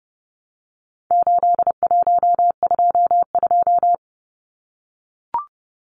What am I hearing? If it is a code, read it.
Morse code, heard as a keyed tone, about 30 wpm, 700 Hz. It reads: 8122
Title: Numbers - 4-Digit 30wpm